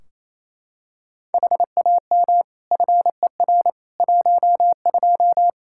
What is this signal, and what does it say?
Morse code, heard as a keyed tone, about 28 wpm, 700 Hz. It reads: HAM FER 12
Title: Practice QSO - Exchange 3 - More Info - 28wpm